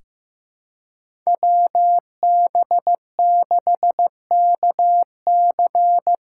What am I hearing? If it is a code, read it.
Morse code, heard as a keyed tone, about 15 wpm, 700 Hz. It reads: WB6KC
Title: USA Call Signs - 6x Character Spacing - 15wpm